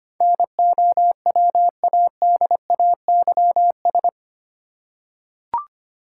Morse code, heard as a keyed tone, about 25 wpm, 690 Hz. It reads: NOWADAYS
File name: 8 Letter Words - First 3 Letters Given - 25wpm